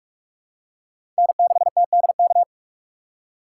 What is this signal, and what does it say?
Morse code, heard as a keyed tone, about 43 wpm, 700 Hz. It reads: N6TDK
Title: USA Call Signs - 5 Characters - 3x Character Spacing - 45wpm